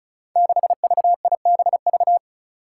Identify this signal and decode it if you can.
Morse code, heard as a keyed tone, about 36 wpm, 700 Hz. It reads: 6VIBV
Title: Five Challenging Letter-Number Combinations 35wpm